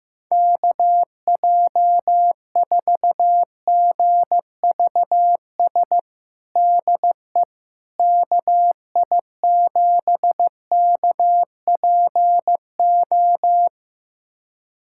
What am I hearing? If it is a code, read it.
Morse code, heard as a keyed tone, about 16 wpm, 700 Hz. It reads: KJ4GVS DE KI7KPO
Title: USA Callsign DE Callsign - 6 Characters - 15wpm